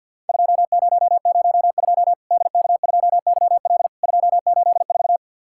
Morse code, heard as a keyed tone, about 50 wpm, 700 Hz. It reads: 2002 DK2YL 284
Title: Worked All Europe DX Contest - QTC Exchange - 50wpm